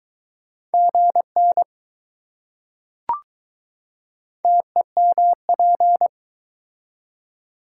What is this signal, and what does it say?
Morse code, heard as a keyed tone, about 21 wpm, 700 Hz. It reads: GN TEMP